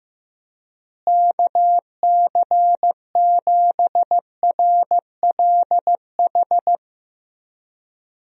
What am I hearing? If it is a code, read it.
Morse code, heard as a keyed tone, about 15 wpm, 700 Hz. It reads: KC7RLH
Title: USA Call Signs - 6 Characters - Repeated 3x - 15wpm